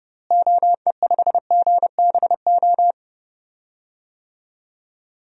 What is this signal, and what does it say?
Morse code, heard as a keyed tone, about 32 wpm, 700 Hz. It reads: OE5GBO
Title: DXCC-100 Call Signs - 6 Characters - Repeated 2x - 30wpm